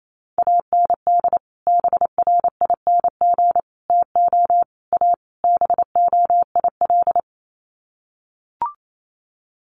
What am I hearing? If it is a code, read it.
Morse code, heard as a keyed tone, about 28 wpm, 700 Hz. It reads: AND BRING TO A BOIL
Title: Sets of 5 Words Encore 28wpm